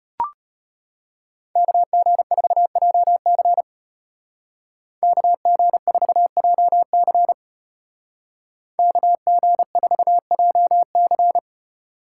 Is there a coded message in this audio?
Yes: Morse code from 1.6 to 11.4 s, heard as a keyed tone, about 35 wpm, 700 Hz. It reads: KG4JC KG4JC KG4JC